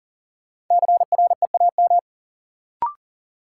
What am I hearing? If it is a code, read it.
Morse code, heard as a keyed tone, about 40 wpm, 700 Hz. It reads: CREAM
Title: Sentences from Top 1500 Words - New Word Drill - 40wpm